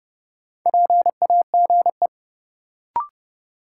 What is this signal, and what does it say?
Morse code, heard as a keyed tone, about 32 wpm, 700 Hz. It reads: PAGE
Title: Sentences from Top 1000 Words - Review Part 1 - New Word Drill - 30wpm